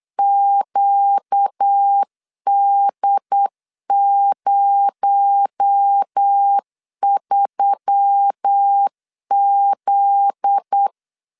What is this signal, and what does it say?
Morse code, heard as a keyed tone, about 8 wpm, 800 Hz. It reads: QD03Z
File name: LEZIONI / 7 GF9Z / 7 GF9Z 8wpm 3213